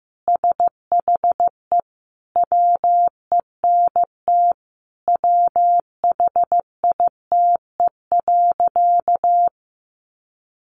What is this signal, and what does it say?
Morse code, heard as a keyed tone, about 14 wpm, 700 Hz. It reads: SHE WENT WHITE.
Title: Sentences from Top 1200 Words 15wpm